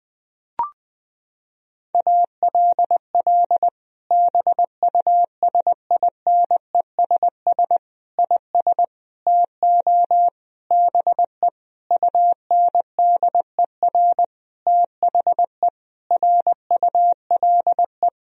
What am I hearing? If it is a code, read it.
Morse code, heard as a keyed tone, about 21 wpm, 700 Hz. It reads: ALL BUSINESS IS TO BE UNDER THE RULE
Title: Sentences from Top 1000 Words - Review Part 3 - 20wpm